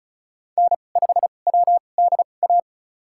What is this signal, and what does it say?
Morse code, heard as a keyed tone, about 36 wpm, spaced as for 25 wpm, 700 Hz. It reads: N5WDA
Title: USA Call Signs - 5 Characters - 2x Character Spacing - 35wpm